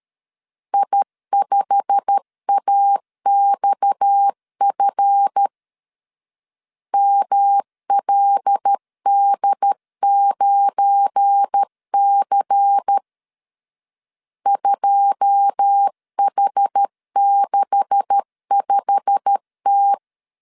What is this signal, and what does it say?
Morse code, heard as a keyed tone, about 13 wpm, 800 Hz. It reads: I5AXF MLD9C 2H65T
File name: LEZIONI / 9 286 12wpm 2305